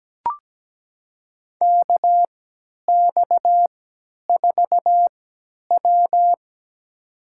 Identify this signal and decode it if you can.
Morse code, heard as a keyed tone, about 17 wpm, spaced as for 10 wpm, 700 Hz. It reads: KX4W
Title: USA Call Signs - 3x Character Spacing - 17wpm